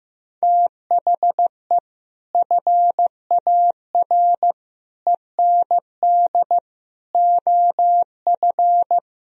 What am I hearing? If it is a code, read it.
Morse code, heard as a keyed tone, about 15 wpm, 700 Hz. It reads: THE FAR END OF
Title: Speed-Racing - Sets of 5 Words 15wpm